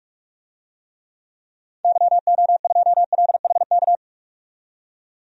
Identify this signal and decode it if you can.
Morse code, heard as a keyed tone, about 46 wpm, 700 Hz. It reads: YO2LHK
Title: DXCC-100 Call Signs - 6 Characters - 45wpm